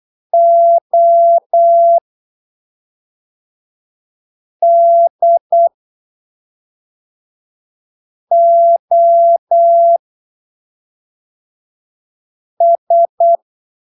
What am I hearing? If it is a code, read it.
Morse code, heard as a keyed tone, about 8 wpm, spaced as for 3 wpm, 680 Hz. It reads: ODOS
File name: arquivos / texto / texto 25 08ppm